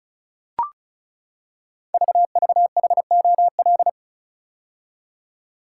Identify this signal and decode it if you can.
Morse code, heard as a keyed tone, about 36 wpm, 700 Hz. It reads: VVHOL
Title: Code Groups - 5 Letters - 35wpm